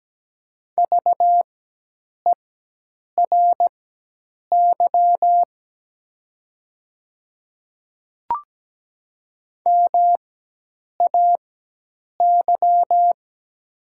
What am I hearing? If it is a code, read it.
Morse code, heard as a keyed tone, about 17 wpm, spaced as for 8 wpm, 700 Hz. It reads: VERY MAY
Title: 4x Character Spacing - Top 200 Words - 17wpm